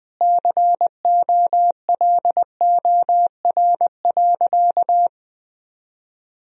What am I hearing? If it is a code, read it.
Morse code, heard as a keyed tone, about 21 wpm, 700 Hz. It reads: COLOR.
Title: Sentences from Top 1000 Words - Review Part 2 - 20wpm